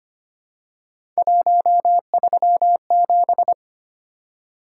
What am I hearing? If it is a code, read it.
Morse code, heard as a keyed tone, about 25 wpm, 700 Hz. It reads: 137
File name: Three Letter-Number Combinations - 25wpm